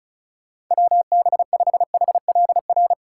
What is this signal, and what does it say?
Morse code, heard as a keyed tone, about 45 wpm, 700 Hz. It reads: WB5HLR